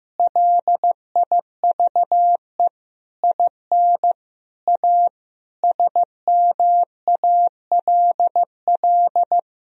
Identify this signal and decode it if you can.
Morse code, heard as a keyed tone, about 15 wpm, 700 Hz. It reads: LIVE IN A SMALL
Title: Sentences from Top 1000 Words - Review Part 1 - 15wpm